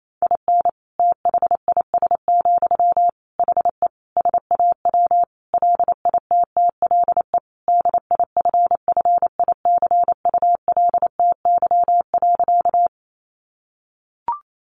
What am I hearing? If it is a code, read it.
Morse code, heard as a keyed tone, about 28 wpm, 700 Hz. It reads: IN THIS, HE SAW LITTLE DIFFICULTY.
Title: Sentences from Top 1500 Words 28wpm